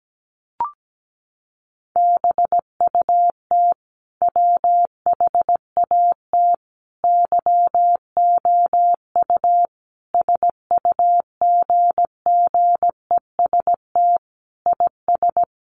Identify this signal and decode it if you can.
Morse code, heard as a keyed tone, about 17 wpm, 700 Hz. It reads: BUT WHAT YOU SUGGEST IS
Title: Sentences from Top 1300 Words 17wpm